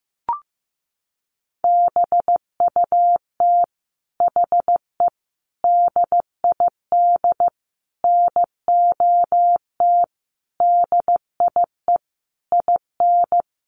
Morse code, heard as a keyed tone, about 15 wpm, 700 Hz. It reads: BUT HE DID NOT DIE IN
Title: Sentences from Top 1300 Words 15wpm